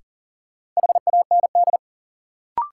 Morse code, heard as a keyed tone, about 40 wpm, 700 Hz. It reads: HAND
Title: Sets of 3 Words - New Word Drill - 40wpm